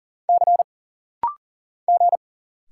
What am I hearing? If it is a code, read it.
Morse code, heard as a keyed tone, about 39 wpm, 700 Hz. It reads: C G
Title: Single Letter-Number - Rapid-Fire 40wpm